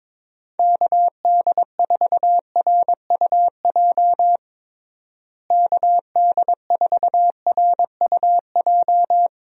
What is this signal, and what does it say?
Morse code, heard as a keyed tone, about 22 wpm, 700 Hz. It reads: KD4RUJ KD4RUJ
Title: USA Call Signs - 6 Characters - Repeated 2x - 22wpm